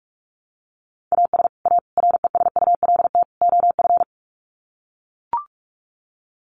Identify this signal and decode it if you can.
Morse code, heard as a keyed tone, about 44 wpm, 700 Hz. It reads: AS A RESULT OF
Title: Sets of 4 Words 45wpm